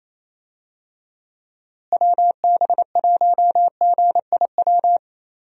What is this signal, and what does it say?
Morse code, heard as a keyed tone, about 28 wpm, 700 Hz. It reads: WB1GIW
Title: USA Call Signs - 6 Characters - Repeated 3x - 28wpm